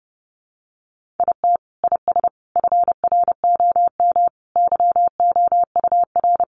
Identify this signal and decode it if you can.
Morse code, heard as a keyed tone, about 30 wpm, 700 Hz. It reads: IT IS FROM YOUR
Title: Sentences from Top 300 Words 30wpm